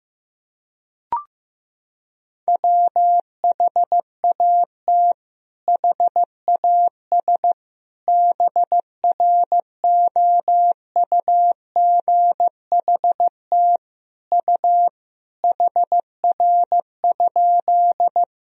Morse code, heard as a keyed tone, about 15 wpm, 700 Hz. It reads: WHAT HAS BROUGHT U HR?
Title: Sentences from Top 500 Words and Abbreviations 15wpm